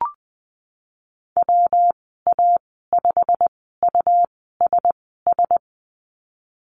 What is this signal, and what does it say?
Morse code, heard as a keyed tone, about 20 wpm, spaced as for 15 wpm, 700 Hz. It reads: WA5USS
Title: 140 Lesson 23g - Letter f - Call Signs 20wpm